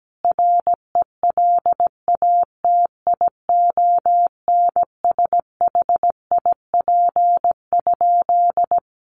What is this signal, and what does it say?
Morse code, heard as a keyed tone, about 17 wpm, 700 Hz. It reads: RELATIONSHIP?
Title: Sentences from Top 1100 Words 17wpm